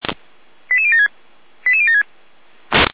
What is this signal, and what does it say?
Five-tone selective call (ZVEI), heard as quick runs of five tones, tones of about 80 ms. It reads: 88076 88076